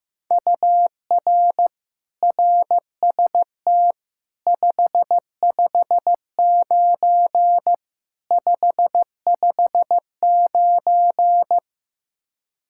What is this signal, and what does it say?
Morse code, heard as a keyed tone, about 15 wpm, 700 Hz. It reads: UR RST 559 559
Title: Practice QSO - Exchange 1 - Beginning - 15wpm